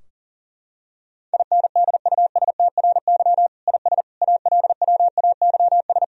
Morse code, heard as a keyed tone, about 40 wpm, 700 Hz. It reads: INDUSTRY IS ALWAYS
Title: Sentences from Top 1000 Words - Review Part 4 - 40wpm